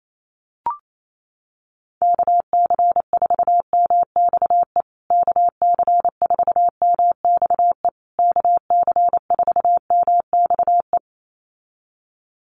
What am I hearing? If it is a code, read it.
Morse code, heard as a keyed tone, about 28 wpm, 700 Hz. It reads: KC4MXE KC4MXE KC4MXE
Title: USA Call Signs Repeated 3x 28wpm